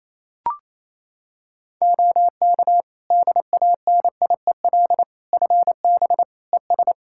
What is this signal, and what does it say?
Morse code, heard as a keyed tone, about 28 wpm, 700 Hz. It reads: OK DANIEL FB ES
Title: K1USN SST QSO Practice 28wpm